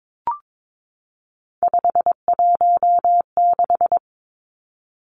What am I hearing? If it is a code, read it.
Morse code, heard as a keyed tone, about 22 wpm, 700 Hz. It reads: 516
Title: Three Challenging Letter-Number Combinations 22wpm